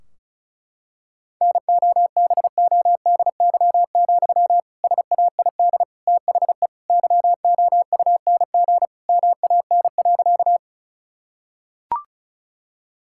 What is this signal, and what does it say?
Morse code, heard as a keyed tone, about 34 wpm, 700 Hz. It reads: NOBODY, SAID THE YOUNG MAN.
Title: Sentences from Top 1600 Words 35wpm